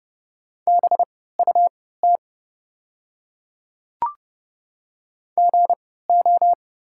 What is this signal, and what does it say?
Morse code, heard as a keyed tone, about 30 wpm, spaced as for 17 wpm, 690 Hz. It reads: BUT GO
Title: 3x Character Spacing - Top 100 Words - 30wpm